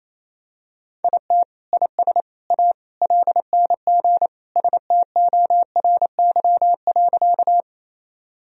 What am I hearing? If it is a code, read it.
Morse code, heard as a keyed tone, about 28 wpm, 700 Hz. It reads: IT IS A LNG STORY.
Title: Sentences from Top 300 Words and Abbreviations 28wpm